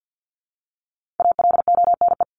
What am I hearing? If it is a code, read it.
Morse code, heard as a keyed tone, about 50 wpm, 700 Hz. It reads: ALONE
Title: Top 751-1000 Words 50wpm